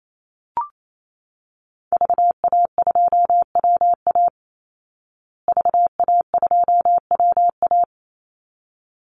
Morse code, heard as a keyed tone, about 28 wpm, 700 Hz. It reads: VA2WA VA2WA
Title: Call Signs Repeated 3x - 28wpm